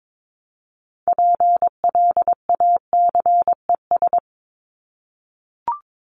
Morse code, heard as a keyed tone, about 22 wpm, 700 Hz. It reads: PLACES
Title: ES:S Suffix - Top 100 Words with Base Words 22wpm